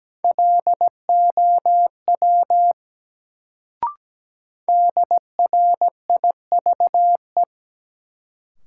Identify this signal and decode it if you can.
Morse code, heard as a keyed tone, about 17 wpm, 700 Hz. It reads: LOW DRIVE
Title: Sentences from Top 500 Words - New Word Drill - 17wpm